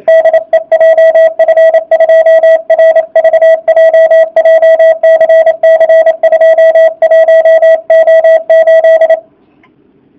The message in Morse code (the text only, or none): DE JF2RVJ JCC 21O8